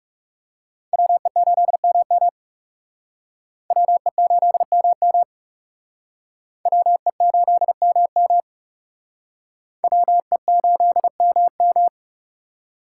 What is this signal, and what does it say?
Morse code, heard as a keyed tone, about 40 wpm, 700 Hz. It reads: WE8MM WE8MM WE8MM WE8MM